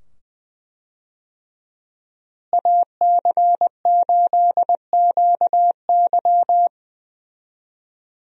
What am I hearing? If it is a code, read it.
Morse code, heard as a keyed tone, about 20 wpm, 700 Hz. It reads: AC8QY
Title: USA Call Signs - 5 Characters - Repeated 2x - 20wpm